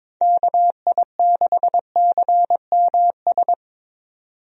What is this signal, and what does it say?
Morse code, heard as a keyed tone, about 22 wpm, 700 Hz. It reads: KI6CMS
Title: USA Call Signs - 6 Characters - 22wpm